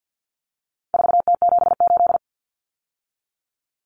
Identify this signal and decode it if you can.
Morse code, heard as a keyed tone, about 50 wpm, 700 Hz. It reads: 4T78
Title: Common Cut Numbers - 4-Digit 50wpm